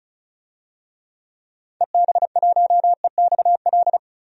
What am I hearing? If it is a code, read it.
Morse code, heard as a keyed tone, about 34 wpm, 700 Hz. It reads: EB1EXL
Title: DXCC-100 Call Signs - 6 Characters - 35wpm